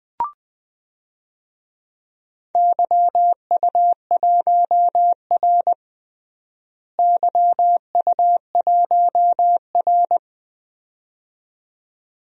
Morse code, heard as a keyed tone, about 20 wpm, 700 Hz. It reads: YU1R YU1R
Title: DXCC-100 Call Signs - 4 Characters - Repeated 2x - 20wpm